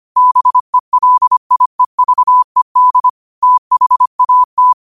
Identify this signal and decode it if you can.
Morse code, heard as a keyed tone, about 25 wpm, 1000 Hz. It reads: BELIEVED THAT
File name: qotdfiles / 25wpm